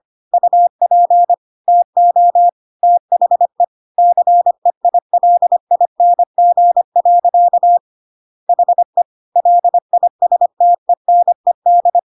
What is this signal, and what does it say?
Morse code, heard as a keyed tone, about 25 wpm, 700 Hz. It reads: UP TO THE CEILING. HE LISTENED
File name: Pinocchio / Pinocchio ch1 25 25 ik1hge